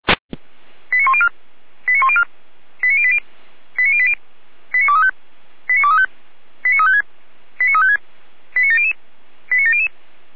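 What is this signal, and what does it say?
Five-tone selective call (ZVEI), heard as quick runs of five tones, tones of about 70 ms. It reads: 89104 89104 89080 89080 89235 89235 89356 89356 89700 89700
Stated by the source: ZVEI five-tone call 89120 received by SDR.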